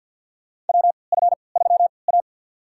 Morse code, heard as a keyed tone, about 53 wpm, spaced as for 29 wpm, 700 Hz. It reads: WF3A
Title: USA Call Signs - 4 Characters - 3x Character Spacing - 50wpm